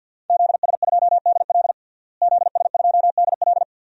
Morse code, heard as a keyed tone, about 51 wpm, 700 Hz. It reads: ZS2DL ZS2DL
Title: DXCC-100 Call Signs - 5 Characters - Repeated 2x - 50wpm